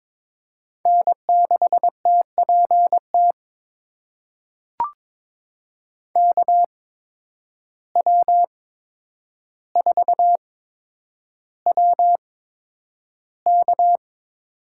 Morse code, heard as a keyed tone, about 22 wpm, spaced as for 9 wpm, 700 Hz. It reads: N6TPT KW4WK
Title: USA Call Signs - 5 Characters - 8x Character Spacing - 22wpm